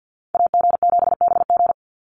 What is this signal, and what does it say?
Morse code, heard as a keyed tone, about 48 wpm, 700 Hz. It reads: AG7BG
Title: USA Call Signs 50wpm